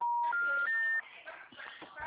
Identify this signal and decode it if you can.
Special information tone, with noise.
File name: out- -2014- - - .4